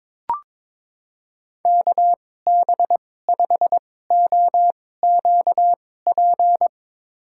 Morse code, heard as a keyed tone, about 22 wpm, spaced as for 16 wpm, 700 Hz. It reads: KB5OQP
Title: USA Call Signs - 6 Characters - 2x Character Spacing - 22wpm